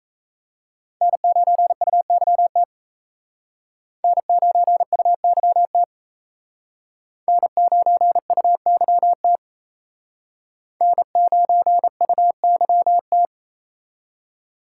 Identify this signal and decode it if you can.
Morse code, heard as a keyed tone, about 37 wpm, 700 Hz. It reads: N9UYT N9UYT N9UYT N9UYT